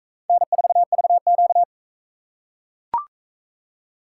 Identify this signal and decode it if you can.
Morse code, heard as a keyed tone, about 44 wpm, 700 Hz. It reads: N4VQ